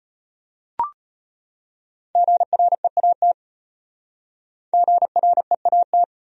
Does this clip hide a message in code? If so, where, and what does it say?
Yes: Morse code from 2.2 to 6.0 s, heard as a keyed tone, about 32 wpm, 700 Hz. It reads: GREAT GREAT